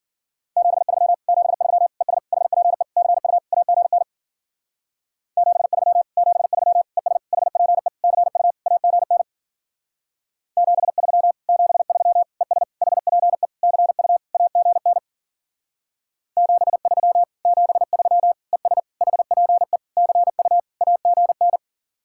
Morse code, heard as a keyed tone, about 48 wpm, 700 Hz. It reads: 73 73 ES HPE CU AGN 73 73 ES HPE CU AGN 73 73 ES HPE CU AGN 73 73 ES HPE CU AGN
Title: Speed-Racing Common QSO Phrases - 40wpm